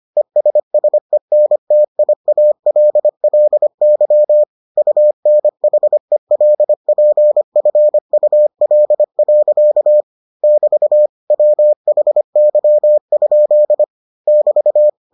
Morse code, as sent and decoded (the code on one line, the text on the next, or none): . ... ... -. - .. .- .-.. .-.. -.-- / ..- -. .... .-.. .--. ..-. ..- .-.. .-.-.- / -...- / .-- .... -.-- ..--.. / -...-
ESSENTIALLY UNHELPFUL. <BT> WHY? <BT>